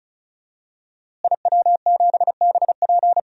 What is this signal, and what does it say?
Morse code, heard as a keyed tone, about 34 wpm, 700 Hz. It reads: IW7BP
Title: DXCC-100 Call Signs - 5 Characters - Repeated 3x - 35wpm